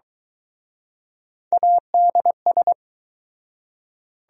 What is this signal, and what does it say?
Morse code, heard as a keyed tone, about 23 wpm, 700 Hz. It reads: ADS